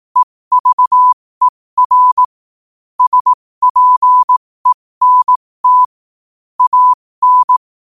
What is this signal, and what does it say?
Morse code, heard as a keyed tone, about 18 wpm, spaced as for 15 wpm, 1000 Hz. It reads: EVER SPENT AN
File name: qotdfiles / 15wpm